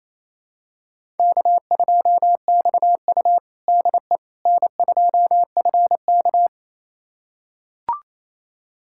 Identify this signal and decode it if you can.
Morse code, heard as a keyed tone, about 28 wpm, 700 Hz. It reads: K2XU DE N2FK
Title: USA Callsign DE Callsign - 4 Characters - 28wpm